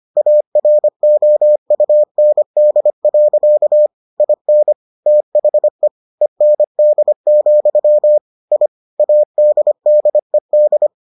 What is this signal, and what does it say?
Morse code, heard as a keyed tone, about 25 wpm, 600 Hz. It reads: AROUND. IN THE END, I ADDED